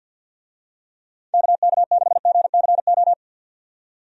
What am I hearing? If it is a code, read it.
Morse code, heard as a keyed tone, about 51 wpm, 700 Hz. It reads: KK6GCO
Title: USA Call Signs - 6 Characters - 50wpm